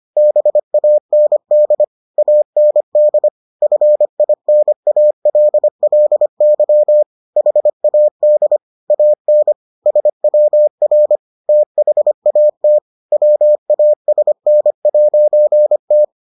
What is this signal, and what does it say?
Morse code, heard as a keyed tone, about 25 wpm, 600 Hz. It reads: BAND AND FINALLY HAD AN SWR THAT WASN'T